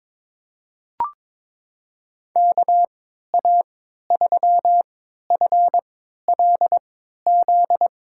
Morse code, heard as a keyed tone, about 22 wpm, spaced as for 13 wpm, 700 Hz. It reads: KA3FLZ
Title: USA Call Signs - 6 Characters - 3x Character Spacing - 22wpm